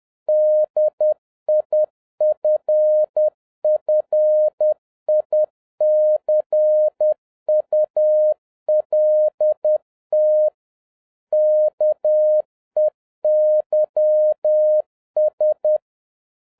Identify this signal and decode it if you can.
Morse code, heard as a keyed tone, about 10 wpm, 610 Hz. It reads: DIFFICULT KEYS